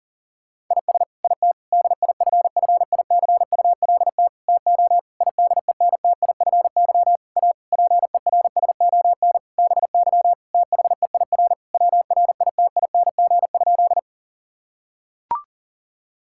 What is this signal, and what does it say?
Morse code, heard as a keyed tone, about 40 wpm, 700 Hz. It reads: IS IT DIFFICULT TO IDENTIFY A PERSON BY THEIR WRITING?
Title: Sentences from Top 1000 Words - Review Part 6 - 40wpm